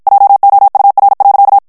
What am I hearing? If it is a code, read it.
Morse code, heard as a keyed tone, about 56 wpm, 800 Hz. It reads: JOUR.
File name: FRANCAIS TEXTE CLAIR 1 VITESSE 50 MOTS MINUTES